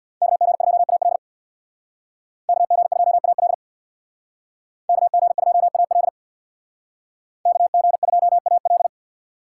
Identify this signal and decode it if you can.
Morse code, heard as a keyed tone, about 59 wpm, 700 Hz. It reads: KG2AL KG2AL KG2AL KG2AL
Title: Speed Racing - USA 2x2 Call Signs 50wpm